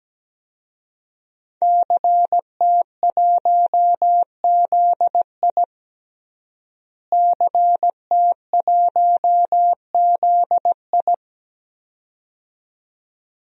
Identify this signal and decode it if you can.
Morse code, heard as a keyed tone, about 17 wpm, 700 Hz. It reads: CT1ZI CT1ZI
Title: DXCC-100 Call Signs - 5 Characters - Repeated 2x - 17wpm